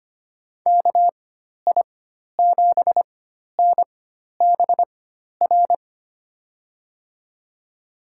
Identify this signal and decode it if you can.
Morse code, heard as a keyed tone, about 25 wpm, spaced as for 12 wpm, 700 Hz. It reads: KI7NBR
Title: USA Call Signs - 6 Characters - 4x Character Spacing - 25wpm